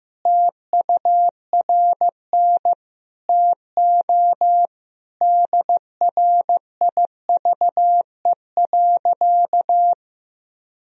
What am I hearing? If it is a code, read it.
Morse code, heard as a keyed tone, about 16 wpm, 700 Hz. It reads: TURN TO DRIVE.
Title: Sentences from Top 1000 Words - Review Part 2 - 15wpm